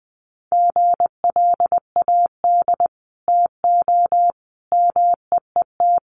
Morse code, heard as a keyed tone, about 20 wpm, 700 Hz. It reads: GLAD TO MEET
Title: Sentences from Top 1100 Words 20wpm